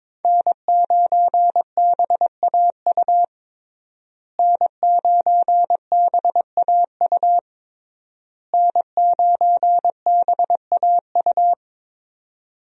Morse code, heard as a keyed tone, about 22 wpm, 700 Hz. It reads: N9BAU N9BAU N9BAU
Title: USA Call Signs - 5 Characters - Repeated 3x - 22wpm